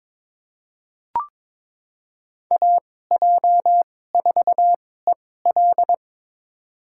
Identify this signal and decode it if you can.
Morse code, heard as a keyed tone, about 22 wpm, spaced as for 16 wpm, 700 Hz. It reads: AJ4EL
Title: USA Call Signs - 5 Characters - 2x Character Spacing - 22wpm